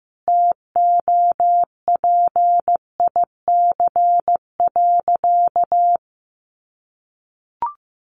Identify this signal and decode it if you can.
Morse code, heard as a keyed tone, about 15 wpm, 700 Hz. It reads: TOPIC.
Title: Sentences from Top 1200 Words 15wpm